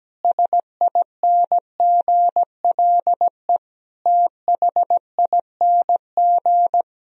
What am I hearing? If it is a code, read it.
Morse code, heard as a keyed tone, about 17 wpm, 700 Hz. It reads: SINGLE THING
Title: Sentences from Top 1000 Words - Review Part 4 - 17wpm